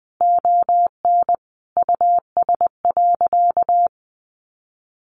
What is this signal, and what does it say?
Morse code, heard as a keyed tone, about 20 wpm, 700 Hz. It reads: ON US.
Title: Sentences from Top 1300 Words 20wpm